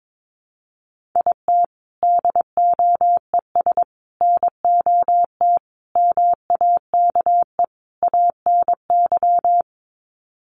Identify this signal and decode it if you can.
Morse code, heard as a keyed tone, about 22 wpm, 700 Hz. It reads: IT DOES NOT MAKE ANY
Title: Sets of 5 Words Encore2 22wpm